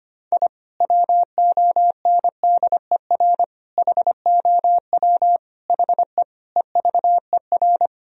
Morse code, heard as a keyed tone, about 25 wpm, 700 Hz. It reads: I WONDER HOW HE EVER
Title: Sentences from Top 1000 Words - Review Part 3 - 25wpm